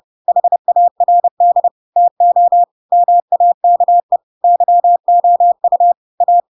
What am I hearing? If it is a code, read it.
Morse code, heard as a keyed tone, about 30 wpm, 700 Hz. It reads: HARD TO MAKE YOU A
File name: Pinocchio / Pinocchio ch3 30 30 ik1hge